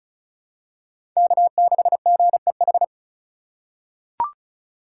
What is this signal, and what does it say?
Morse code, heard as a keyed tone, about 36 wpm, 700 Hz. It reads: K6GEH
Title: USA Call Signs - 3x Character Spacing - 35wpm